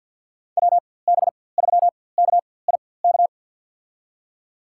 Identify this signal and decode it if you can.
Morse code, heard as a keyed tone, about 53 wpm, spaced as for 24 wpm, 700 Hz. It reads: WB3KIK
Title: USA Call Signs - 4x Character Spacing - 50wpm